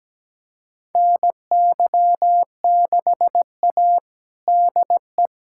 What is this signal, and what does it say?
Morse code, heard as a keyed tone, about 17 wpm, 700 Hz. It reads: NY6A DE
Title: USA Callsign DE Callsign - 4 Characters - 17wpm